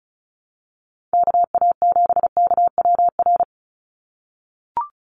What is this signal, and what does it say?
Morse code, heard as a keyed tone, about 36 wpm, 700 Hz. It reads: KA7KWR
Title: USA Call Signs 35wpm